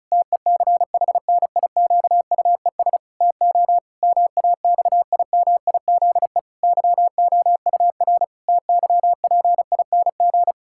Morse code, heard as a keyed tone, about 34 wpm, 700 Hz. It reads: TECHNIQUES TO MAXIMIZE YOUR TYPING
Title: Pangram Challenge v2 - 35wpm